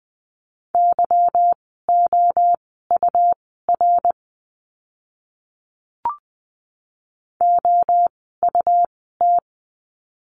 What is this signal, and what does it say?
Morse code, heard as a keyed tone, about 20 wpm, spaced as for 14 wpm, 700 Hz. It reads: YOUR OUT
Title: 244 Lesson 39f - Letter z - Words 20wpm